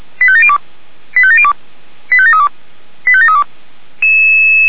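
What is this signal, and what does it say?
Five-tone selective call (ZVEI), heard as quick runs of five tones, tones of about 70 ms. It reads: 85702 85702 85732 85732
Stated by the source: ZVEI five-tone selective call 89145 received by SDR.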